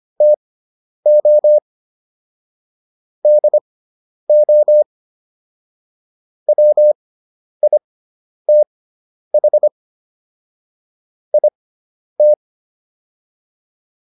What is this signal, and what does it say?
Morse code, heard as a keyed tone, about 25 wpm, spaced as for 10 wpm, 600 Hz. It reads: TO DO WITH IT
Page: intermediate-practice-files